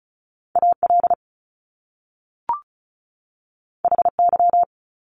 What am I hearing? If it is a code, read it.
Morse code, heard as a keyed tone, about 36 wpm, 700 Hz. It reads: AL HY
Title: Top Two Letter Combinations in Words 35wpm